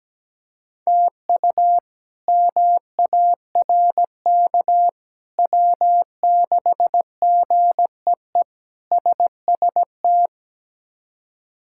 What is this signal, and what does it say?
Morse code, heard as a keyed tone, about 17 wpm, 700 Hz. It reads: TU MARK W6GEE SST
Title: K1USN SST QSO Practice - 17wpm